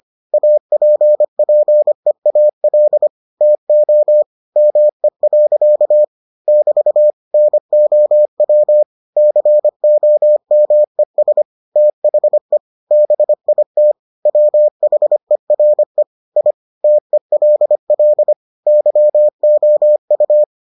Morse code, heard as a keyed tone, about 25 wpm, 600 Hz. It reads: APPEAL TO ME. <BT> NOW COMES THE BIT WHERE I TELL YOU